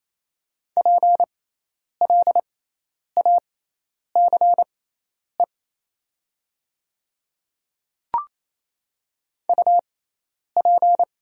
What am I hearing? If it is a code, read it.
Morse code, heard as a keyed tone, about 27 wpm, spaced as for 10 wpm, 700 Hz. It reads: PLACE UP
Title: 6x Character Spacing - Top 200 Words - 28wpm